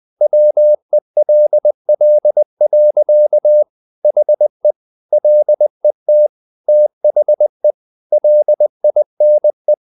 Morse code, heard as a keyed tone, about 20 wpm, 600 Hz. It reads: WELL. HE LET THE LINE
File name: the old man and the sea / the old man and the sea part 1 20wpm